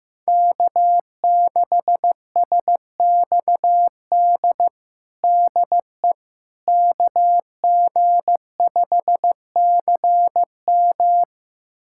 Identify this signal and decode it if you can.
Morse code, heard as a keyed tone, about 15 wpm, 700 Hz. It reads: K6SXD DE KG5CM
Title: USA Callsign DE Callsign - 5 Characters - 15wpm